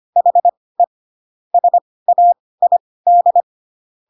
Morse code, heard as a keyed tone, about 25 wpm, spaced as for 18 wpm, 700 Hz. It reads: HE SAID
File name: Pinocchio ch3 12 25 ik1hge